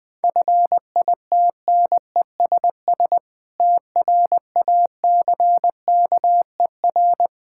Morse code, heard as a keyed tone, about 20 wpm, 700 Hz. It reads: FITNESS TRACKER
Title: Difficult Words with Hints - 20wpm